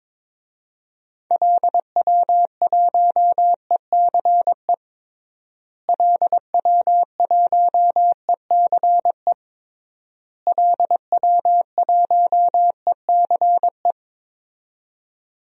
Morse code, heard as a keyed tone, about 22 wpm, 700 Hz. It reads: LW1ECE LW1ECE LW1ECE
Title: DXCC-100 Call Signs - 6 Characters - Repeated 3x - 22wpm